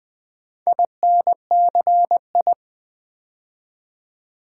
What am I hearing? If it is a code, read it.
Morse code, heard as a keyed tone, about 20 wpm, 700 Hz. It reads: INCI
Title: Build A Word - 2 Letters At A Time - 8-Letter Words - 20wpm